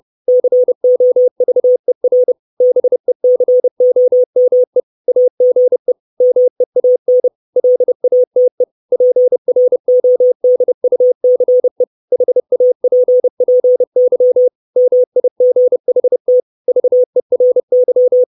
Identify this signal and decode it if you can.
Morse code, heard as a keyed tone, about 30 wpm, 500 Hz. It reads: COVER BECOME AGE MEAN LATE PRODUCE HAPPY MIGHT VERY
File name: 632 Words 30WPM